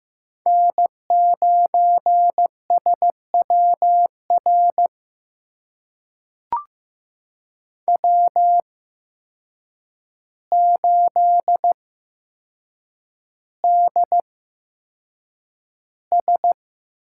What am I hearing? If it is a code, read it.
Morse code, heard as a keyed tone, about 15 wpm, spaced as for 7 wpm, 700 Hz. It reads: N9SWR W8DS
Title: USA Call Signs - 5 Characters - 8x Character Spacing - 15wpm